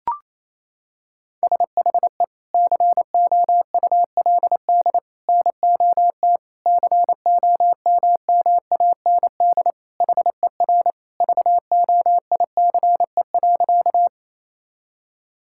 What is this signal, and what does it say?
Morse code, heard as a keyed tone, about 28 wpm, 700 Hz. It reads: SHE COULD NOT COMMAND HER VOICE.
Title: Sentences from Top 1600 Words 28wpm